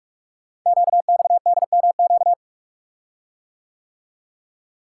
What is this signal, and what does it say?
Morse code, heard as a keyed tone, about 44 wpm, 700 Hz. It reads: QXDMQ
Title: Code Groups - 5 Letters - 45wpm